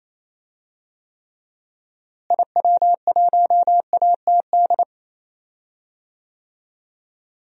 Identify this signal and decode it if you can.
Morse code, heard as a keyed tone, about 28 wpm, 700 Hz. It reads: IW1ATD
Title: DXCC-100 Call Signs - 6 Characters - Repeated 3x - 28wpm